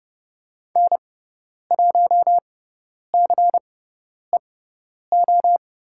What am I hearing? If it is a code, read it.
Morse code, heard as a keyed tone, about 30 wpm, spaced as for 10 wpm, 700 Hz. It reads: N1CEO
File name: Lesson 17g - Letter u - Call Signs 30wpm